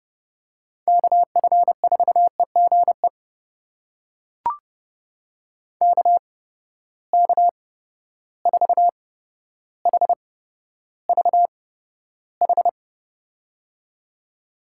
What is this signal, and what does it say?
Morse code, heard as a keyed tone, about 30 wpm, spaced as for 13 wpm, 700 Hz. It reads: KF4EGE KK4HVH
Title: USA Call Signs - 6 Characters - 8x Character Spacing - 30wpm